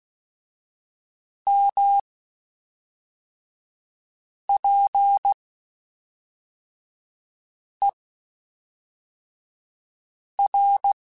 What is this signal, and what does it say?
Morse code, heard as a keyed tone, about 16 wpm, 800 Hz. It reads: M P E R
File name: Spaced Characters / Spaced Characters-mp3s / Weather4 14 sp @5